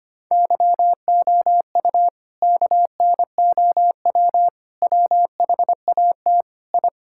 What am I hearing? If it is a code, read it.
Morse code, heard as a keyed tone, about 25 wpm, 700 Hz. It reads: YOU KNOW WHAT I
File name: practice / 25wpm / Sentences with 4-Word Phrases - 25wpm